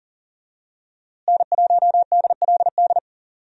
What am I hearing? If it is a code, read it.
Morse code, heard as a keyed tone, about 40 wpm, 700 Hz. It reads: N1DLD
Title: USA Call Signs - 5 Characters - Repeated 2x - 40wpm